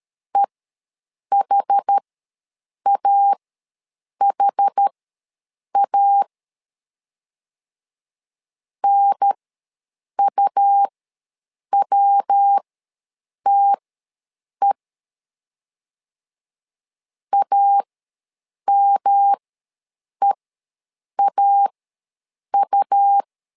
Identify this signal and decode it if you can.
Morse code, heard as a keyed tone, about 13 wpm, spaced as for 7 wpm, 800 Hz. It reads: EHAHA NUWTE AMEAU
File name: LEZIONI / 3 MWRO / 3 MW ws 12wpm 1816